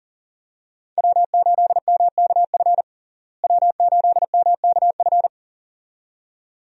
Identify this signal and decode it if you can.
Morse code, heard as a keyed tone, about 40 wpm, 700 Hz. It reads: W8MKF W8MKF
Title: USA Call Signs - 5 Characters - Repeated 2x - 40wpm